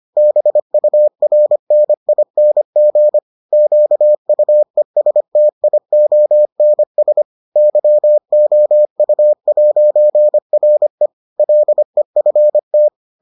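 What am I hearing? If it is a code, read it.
Morse code, heard as a keyed tone, about 25 wpm, 600 Hz. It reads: BURNING QUESTIONS YOU'RE LEFT
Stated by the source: Morse edition of the Foundations of Amateur Radio episodes.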